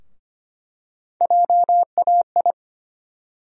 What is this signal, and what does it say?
Morse code, heard as a keyed tone, about 25 wpm, 700 Hz. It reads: JAI
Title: USA Call Sign Postfixes - 3 Letters 25wpm